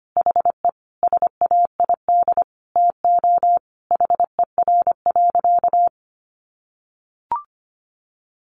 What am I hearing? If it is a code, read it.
Morse code, heard as a keyed tone, about 25 wpm, 700 Hz. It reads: HE SAID TO HER.
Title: Sentences from Top 1500 Words 25wpm